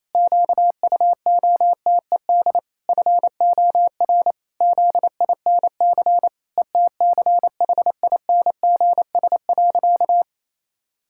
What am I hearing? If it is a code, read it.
Morse code, heard as a keyed tone, about 28 wpm, 700 Hz. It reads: QUOTED FOR ZINC ETCHINGS.
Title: Pangram Challenge v2 - 28wpm